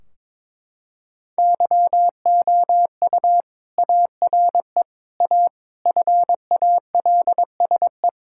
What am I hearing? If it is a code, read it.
Morse code, heard as a keyed tone, about 22 wpm, 700 Hz. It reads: YOU ARE A FALSE
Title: Sentences from Top 1600 Words 22wpm